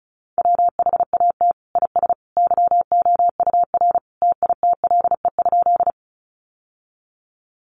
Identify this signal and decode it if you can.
Morse code, heard as a keyed tone, about 34 wpm, 700 Hz. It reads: WHAT IS YOUR TITLE?
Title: Sentences from Top 1200 Words 35wpm